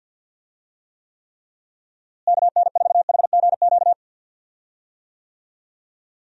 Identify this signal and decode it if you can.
Morse code, heard as a keyed tone, about 50 wpm, 700 Hz. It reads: KN4HGQ
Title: USA Call Signs - 6 Characters - Repeated 3x - 50wpm